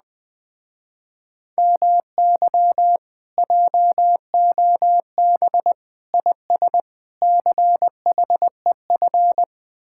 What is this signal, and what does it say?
Morse code, heard as a keyed tone, about 20 wpm, 700 Hz. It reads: MY JOB IS CHEF
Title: Speed-Racing Common QSO Phrases Encore - 17wpm